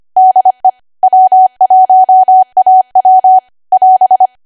Morse code, heard as a keyed tone, about 25 wpm, 750 Hz. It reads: DE W1AW <AS>